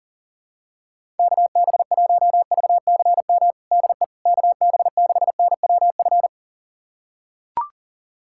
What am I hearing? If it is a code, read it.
Morse code, heard as a keyed tone, about 40 wpm, 700 Hz. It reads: KB1VCM DE KB6NWF
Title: USA Callsign DE Callsign - 6 Characters - 40wpm